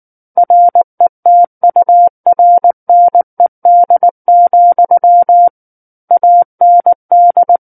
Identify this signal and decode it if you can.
Morse code, heard as a keyed tone, about 19 wpm, 700 Hz. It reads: RETURNED, AND
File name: Book One / 10-Chapter 10 - 19 WPM